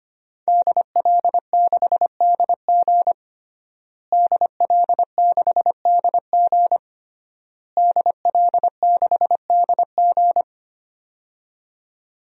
Morse code, heard as a keyed tone, about 25 wpm, 700 Hz. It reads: DL6DG DL6DG DL6DG
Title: DXCC-100 Call Signs - 5 Characters - Repeated 3x - 25wpm